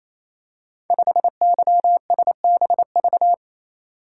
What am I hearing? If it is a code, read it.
Morse code, heard as a keyed tone, about 27 wpm, 700 Hz. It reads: 5YSBV
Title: Code Groups - 5 Letter:Numbers - 28wpm